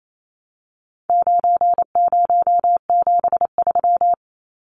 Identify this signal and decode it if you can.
Morse code, heard as a keyed tone, about 28 wpm, 700 Hz. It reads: 9073
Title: Numbers - 4-Digit 28wpm